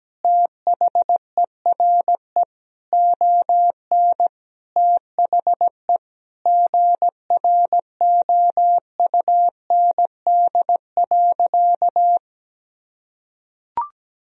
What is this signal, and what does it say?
Morse code, heard as a keyed tone, about 17 wpm, 700 Hz. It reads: THERE ON THE GROUND.
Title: Sentences from Top 1000 Words - Review Part 2 - 17wpm